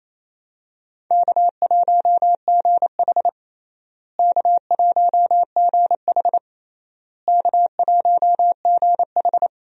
Morse code, heard as a keyed tone, about 28 wpm, 700 Hz. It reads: K1GH K1GH K1GH
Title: USA Call Signs - 4 Characters - Repeated 3x - 28wpm